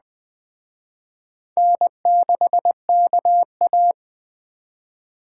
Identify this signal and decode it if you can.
Morse code, heard as a keyed tone, about 21 wpm, 700 Hz. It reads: N6KA